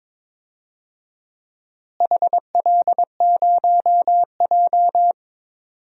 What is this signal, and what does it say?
Morse code, heard as a keyed tone, about 21 wpm, 700 Hz. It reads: HL0J
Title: DXCC-100 Call Signs - 4 Characters - Repeated 3x - 22wpm